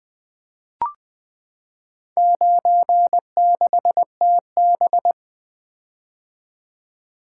Morse code, heard as a keyed tone, about 20 wpm, 700 Hz. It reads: 96TB
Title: Four Letter-Number Combinations - 20wpm